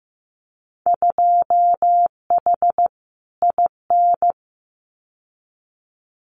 Morse code, heard as a keyed tone, about 15 wpm, 700 Hz. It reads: 2H IN
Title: Winter Field Day Exchanges Encore 15wpm